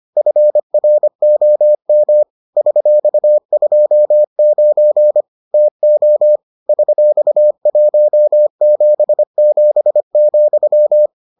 Morse code, heard as a keyed tone, about 25 wpm, 600 Hz. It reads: FROM $29 TO $177,